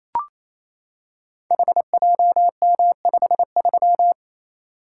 Morse code, heard as a keyed tone, about 28 wpm, 700 Hz. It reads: HJM53
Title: Code Groups - 5 Letter:Numbers - 28wpm